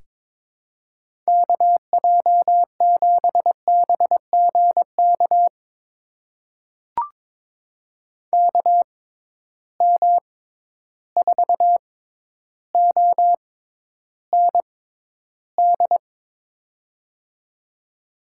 Morse code, heard as a keyed tone, about 22 wpm, spaced as for 11 wpm, 700 Hz. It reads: KJ7BGK KM4OND
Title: USA Call Signs - 6 Characters - 6x Character Spacing - 22wpm